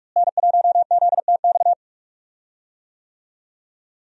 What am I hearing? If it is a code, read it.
Morse code, heard as a keyed tone, about 43 wpm, 700 Hz. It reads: N1ZTX
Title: USA Call Signs - 5 Characters - Repeated 3x - 45wpm